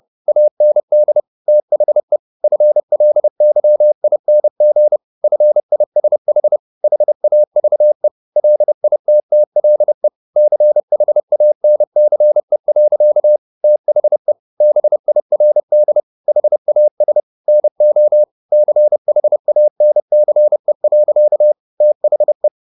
Morse code, heard as a keyed tone, about 30 wpm, 600 Hz. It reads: AND THE FLYING FISH HAVE LITTLE CHANCE. THE BIRD HAS NO CHANCE. THE
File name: the old man and the sea / the old man and the sea 1 30wpm